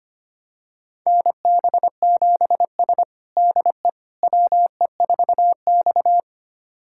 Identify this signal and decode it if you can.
Morse code, heard as a keyed tone, about 25 wpm, 700 Hz. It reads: NB7S DE WE4X
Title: USA Callsign DE Callsign - 4 Characters - 25wpm